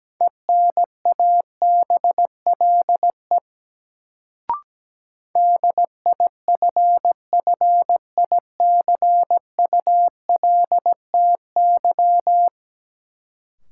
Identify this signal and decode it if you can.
Morse code, heard as a keyed tone, about 17 wpm, 700 Hz. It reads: ENABLE DIFFICULTY
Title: Sentences from Top 1500 Words - New Word Drill - 17wpm